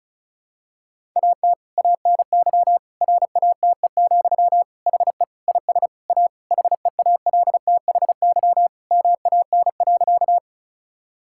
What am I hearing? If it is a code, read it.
Morse code, heard as a keyed tone, about 36 wpm, 700 Hz. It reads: AT ANY RATE, HE IS A HEALTHY MAN.
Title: Sentences from Top 1000 Words - Review Part 5 - 35wpm